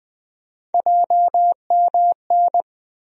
Morse code, heard as a keyed tone, about 20 wpm, 700 Hz. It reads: JMN
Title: USA Call Sign Postfixes - 3 Letters 20wpm